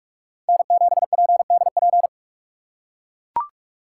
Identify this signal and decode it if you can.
Morse code, heard as a keyed tone, about 42 wpm, 700 Hz. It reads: N7PDP
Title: USA Call Signs - 3x Character Spacing - 45wpm